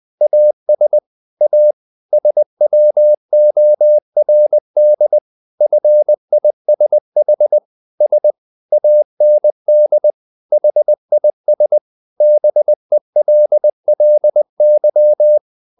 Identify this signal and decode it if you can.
Morse code, heard as a keyed tone, about 20 wpm, 600 Hz. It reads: AS A SWORD FISH S AND HIS BELLY
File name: the old man and the sea part 2 20wpm